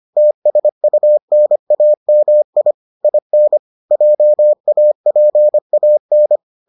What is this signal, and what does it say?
Morse code, heard as a keyed tone, about 25 wpm, 600 Hz. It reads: TSUNAMI IN JAPAN